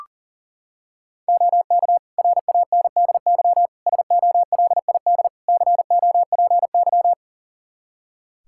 Morse code, heard as a keyed tone, about 37 wpm, 700 Hz. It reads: OK RANDY SOLID COPY
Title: Practice QSO - Exchange 3 - More Info - 40wpm